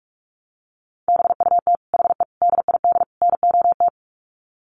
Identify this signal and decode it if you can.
Morse code, heard as a keyed tone, about 44 wpm, 700 Hz. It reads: BUT HE DID NOT
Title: Sets of 4 Words 45wpm